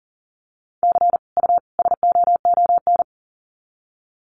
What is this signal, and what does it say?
Morse code, heard as a keyed tone, about 40 wpm, 700 Hz. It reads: C U SOON
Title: Common QSO Phrases 40wpm